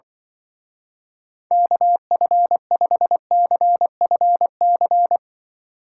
Morse code, heard as a keyed tone, about 25 wpm, 700 Hz. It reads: KF5CFC
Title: Speed Racing - USA 2x3 Call Signs 20wpm